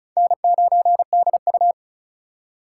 Morse code, heard as a keyed tone, about 34 wpm, 700 Hz. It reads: N9DU
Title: USA Call Signs - 4 Characters - 6x Character Spacing - 35wpm